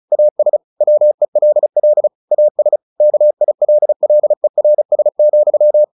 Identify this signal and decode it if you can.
Morse code, heard as a keyed tone, about 34 wpm, 600 Hz. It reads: AS WELL AS KILLERS,
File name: the old man and the sea 2 35wpm